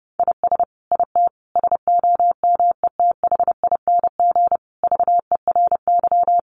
Morse code, heard as a keyed tone, about 30 wpm, 700 Hz. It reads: IS IT SOMETHING VERY
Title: Sentences from Top 700 Words 30wpm